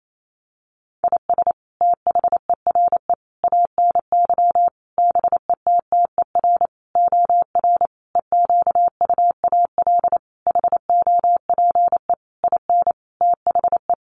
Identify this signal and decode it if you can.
Morse code, heard as a keyed tone, about 28 wpm, 700 Hz. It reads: IS THERE ANY BETTER OR EQUAL HOPE IN THE
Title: Sentences from Top 1400 Words 28wpm